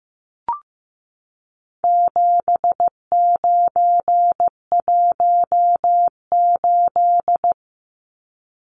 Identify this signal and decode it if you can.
Morse code, heard as a keyed tone, about 15 wpm, 700 Hz. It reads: 7918
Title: Numbers - 4-Digit 15wpm